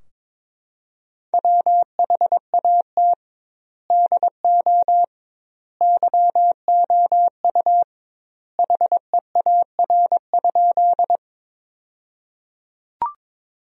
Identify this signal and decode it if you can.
Morse code, heard as a keyed tone, about 22 wpm, 700 Hz. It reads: WHAT DO YOU HEAR?
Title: 2x Word-Spacing - Sentences from Top 300 Words - 22wpm